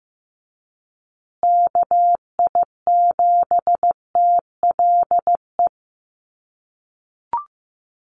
Morse code, heard as a keyed tone, about 15 wpm, 700 Hz. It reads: KI7TLE
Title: USA Call Signs Repeated 2x 15wpm